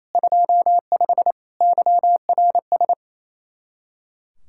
Morse code, heard as a keyed tone, about 28 wpm, 700 Hz. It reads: 25 YRS
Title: K1USN SST QSO Practice 28wpm